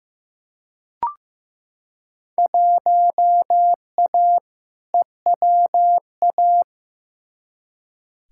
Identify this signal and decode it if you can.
Morse code, heard as a keyed tone, about 15 wpm, 700 Hz. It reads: 1A EWA
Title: Field Day Exchanges Encore - 15wpm